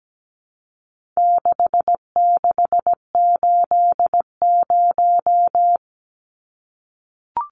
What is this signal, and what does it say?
Morse code, heard as a keyed tone, about 17 wpm, 700 Hz. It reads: 6680
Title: Numbers - 4-Digit 17wpm